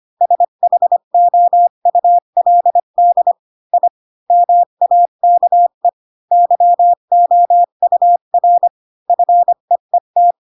Morse code, heard as a keyed tone, about 25 wpm, 700 Hz. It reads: SHOULD I MAKE YOUR FEET
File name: Pinocchio / Pinocchio ch8 20 25 ik1hge